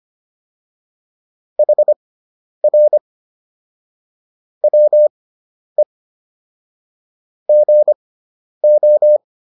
Morse code, heard as a keyed tone, about 25 wpm, spaced as for 10 wpm, 600 Hz. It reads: HR WE GO